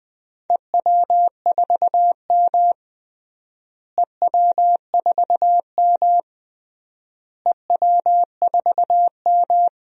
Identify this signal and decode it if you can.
Morse code, heard as a keyed tone, about 21 wpm, 700 Hz. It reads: EW4M EW4M EW4M
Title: DXCC-100 Call Signs - 4 Characters - Repeated 3x - 20wpm